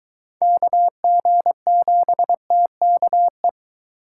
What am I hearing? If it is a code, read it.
Morse code, heard as a keyed tone, about 23 wpm, 700 Hz. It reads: KG7TKE